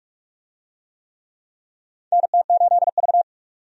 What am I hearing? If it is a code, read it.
Morse code, heard as a keyed tone, about 41 wpm, 700 Hz. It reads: NT8V
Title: USA Call Signs - 4 Characters - 45wpm